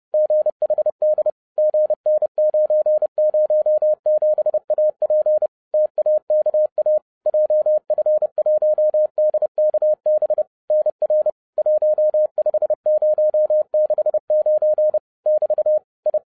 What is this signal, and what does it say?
Morse code, heard as a keyed tone, about 30 wpm, 610 Hz. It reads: GHD GN907AP TAKA JF1DKB NR 15069 <BT> I